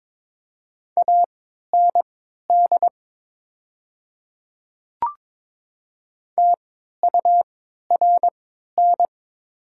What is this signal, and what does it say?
Morse code, heard as a keyed tone, about 22 wpm, spaced as for 13 wpm, 700 Hz. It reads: AND TURN
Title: 3x Character Spacing - Top 300 Words - 22wpm